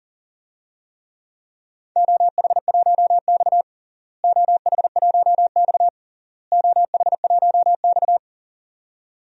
Morse code, heard as a keyed tone, about 40 wpm, 700 Hz. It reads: OH1X OH1X OH1X
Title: DXCC-100 Call Signs - 4 Characters - Repeated 3x - 40wpm